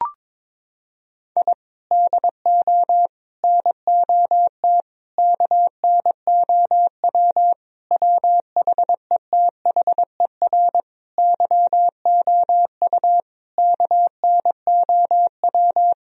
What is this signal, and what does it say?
Morse code, heard as a keyed tone, about 22 wpm, 700 Hz. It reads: I DO NOT KNOW WHETHER YOU KNOW
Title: Sentences from Top 1000 Words - Review Part 3 - 22wpm